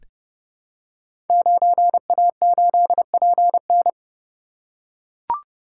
Morse code, heard as a keyed tone, about 30 wpm, 700 Hz. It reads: 9A8PN
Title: Code Groups - 5 Letter:Numbers - 30wpm